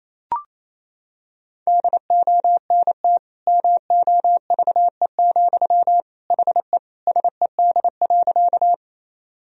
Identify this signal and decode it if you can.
Morse code, heard as a keyed tone, about 28 wpm, 700 Hz. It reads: DONT MOVE, HE SED.
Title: Sentences from Top 300 Words and Abbreviations 28wpm